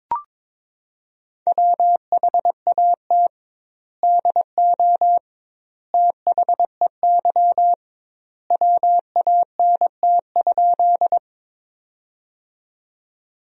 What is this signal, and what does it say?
Morse code, heard as a keyed tone, about 21 wpm, 700 Hz. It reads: WHAT DO THEY WANT?
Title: 2x Word-Spacing - Sentences from Top 200 Words - 22wpm